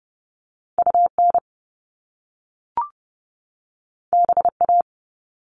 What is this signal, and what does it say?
Morse code, heard as a keyed tone, about 30 wpm, 700 Hz. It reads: UN BA
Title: Top Two Letter Combinations in Words 30wpm